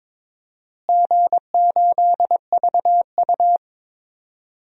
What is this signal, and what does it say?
Morse code, heard as a keyed tone, about 22 wpm, 700 Hz. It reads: G8VU
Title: DXCC-100 Call Signs - 4 Characters - Repeated 2x - 22wpm